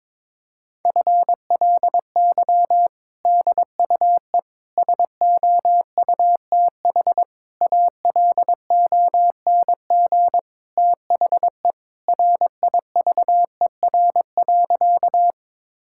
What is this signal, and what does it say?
Morse code, heard as a keyed tone, about 22 wpm, 700 Hz. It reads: FLY DUE SOUTH ALONG THE RIVER.
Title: Sentences from Top 1000 Words - Review Part 6 - 22wpm